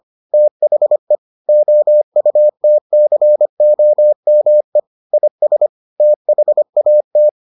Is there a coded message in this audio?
Yes: Morse code from 0.3 to 7.3 s, heard as a keyed tone, about 25 wpm, 600 Hz. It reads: THE OUTCOME IS THAT